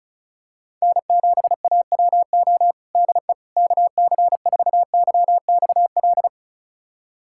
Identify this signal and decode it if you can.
Morse code, heard as a keyed tone, about 36 wpm, 700 Hz. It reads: N7AWO DE KC4YXL
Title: USA Callsign DE Callsign 35wpm